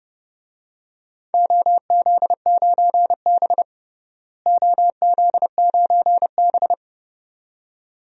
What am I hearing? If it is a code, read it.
Morse code, heard as a keyed tone, about 30 wpm, 700 Hz. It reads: OZ9B OZ9B
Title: DXCC-100 Call Signs - 4 Characters - Repeated 2x - 30wpm